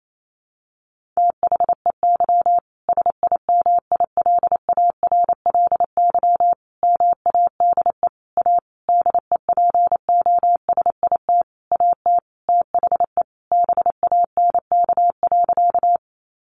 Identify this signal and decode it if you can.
Morse code, heard as a keyed tone, about 28 wpm, 700 Hz. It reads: THEY SIMILARLY MADE A DEPOSIT AT THE BANK.
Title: Sentences from Top 1400 Words 28wpm